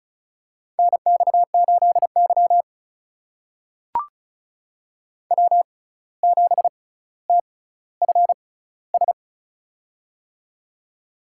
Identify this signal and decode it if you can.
Morse code, heard as a keyed tone, about 34 wpm, spaced as for 17 wpm, 700 Hz. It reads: NX8Y W7TFS
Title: USA Call Signs - 6x Character Spacing - 35wpm